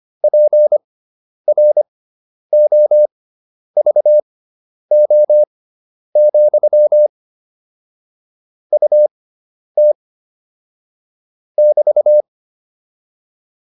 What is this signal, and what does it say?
Morse code, heard as a keyed tone, about 25 wpm, spaced as for 10 wpm, 600 Hz. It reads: PROVO, UT <BT>
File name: QSO301 10